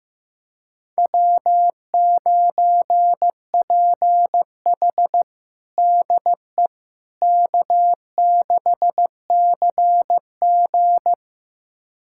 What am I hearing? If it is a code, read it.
Morse code, heard as a keyed tone, about 15 wpm, 700 Hz. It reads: W9PH DE K6CG
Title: USA Callsign DE Callsign - 4 Characters - 15wpm